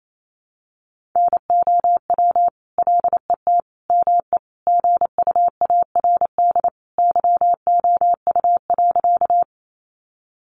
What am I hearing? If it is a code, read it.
Morse code, heard as a keyed tone, about 28 wpm, 700 Hz. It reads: NOW LET ME GUARD YOU.
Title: Sentences from Top 700 Words 28wpm